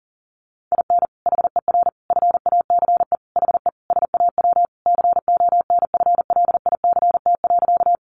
Morse code, heard as a keyed tone, about 40 wpm, 700 Hz. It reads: IN HER FACE HE SAW CONFLICT.
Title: Sentences from Top 1300 Words 40wpm